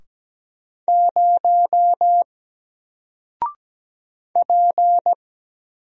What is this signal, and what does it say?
Morse code, heard as a keyed tone, about 17 wpm, 700 Hz. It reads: 0 P
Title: Single Letter-Number Alt 17wpm